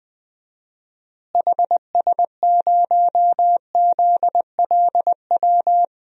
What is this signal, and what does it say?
Morse code, heard as a keyed tone, about 20 wpm, 700 Hz. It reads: HS0ZLW
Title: DXCC-100 Call Signs - 6 Characters - Repeated 2x - 20wpm